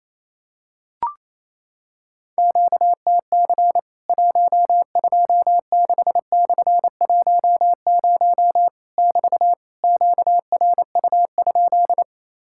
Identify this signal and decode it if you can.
Morse code, heard as a keyed tone, about 28 wpm, 700 Hz. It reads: QTC 126/10 <BT> QRU?
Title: Worked All Europe DX Contest - QTC Exchange - 28wpm